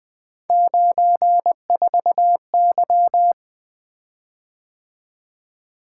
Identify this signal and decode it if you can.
Morse code, heard as a keyed tone, about 20 wpm, 700 Hz. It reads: 94Y
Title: Three Letter-Number Combinations - 20wpm